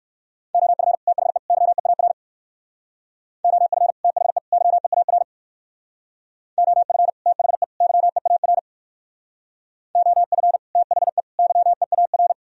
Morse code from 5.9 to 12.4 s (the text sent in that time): OF THE YEAR OF THE YEAR